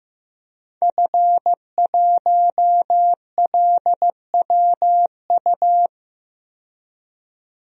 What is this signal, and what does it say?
Morse code, heard as a keyed tone, about 15 wpm, 700 Hz. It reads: F1LWU
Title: DXCC-100 Call Signs - 5 Characters - 15wpm